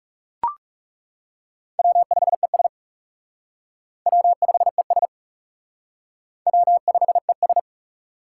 Morse code, heard as a keyed tone, about 36 wpm, 700 Hz. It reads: W5ES W5ES W5ES
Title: Speed Racing - USA 1x2 Call Signs 30wpm